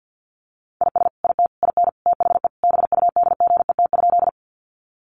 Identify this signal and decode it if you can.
Morse code, heard as a keyed tone, about 50 wpm, 700 Hz. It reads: IS IT IN THE BUDGET?
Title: Sentences from Top 1200 Words 50wpm